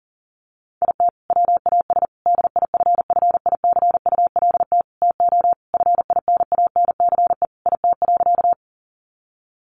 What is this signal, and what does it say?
Morse code, heard as a keyed tone, about 40 wpm, 700 Hz. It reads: IT WAS DIFFICULT TO FINANCE IT.
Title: Sentences from Top 1500 Words 40wpm